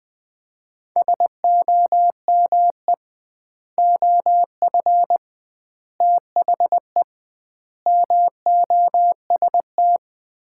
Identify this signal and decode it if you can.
Morse code, heard as a keyed tone, about 21 wpm, 700 Hz. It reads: SOME OF THE MOST
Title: 2x Word-Spacing - Sets of 4 Words - 20wpm